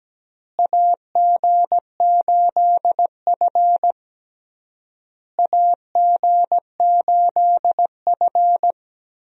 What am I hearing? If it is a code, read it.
Morse code, heard as a keyed tone, about 17 wpm, 700 Hz. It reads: AG8F AG8F
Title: USA Call Signs - 4 Characters - Repeated 3x - 17wpm